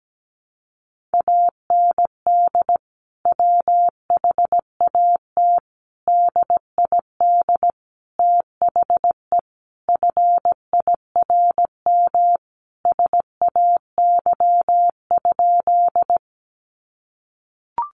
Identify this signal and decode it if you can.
Morse code, heard as a keyed tone, about 17 wpm, 700 Hz. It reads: AND WHAT DID THE FIRM SAY?
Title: Sentences from Top 1400 Words 17wpm